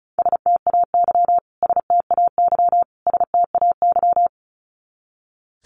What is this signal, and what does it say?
Morse code, heard as a keyed tone, about 34 wpm, 700 Hz. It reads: STAY STAY STAY
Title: Top 500 Words Repeated 35wpm